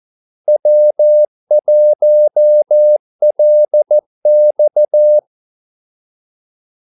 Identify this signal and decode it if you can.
Morse code, heard as a keyed tone, about 14 wpm, 600 Hz. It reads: W1LX
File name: Callsigns-14wpm